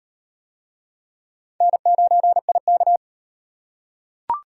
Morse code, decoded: N9IK

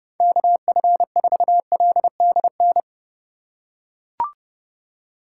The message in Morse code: KF4LDN

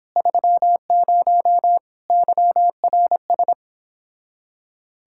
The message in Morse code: 30 YRS

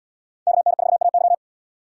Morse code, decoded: KN4AY

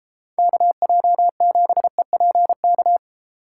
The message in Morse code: KJ7EPK

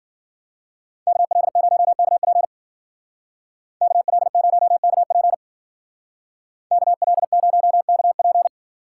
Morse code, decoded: KL0KP KL0KP KL0KP

